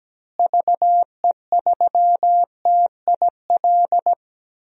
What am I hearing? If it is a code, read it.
Morse code, heard as a keyed tone, about 17 wpm, 700 Hz. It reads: VE3TIL
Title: DXCC-100 Call Signs - 6 Characters - Repeated 3x - 17wpm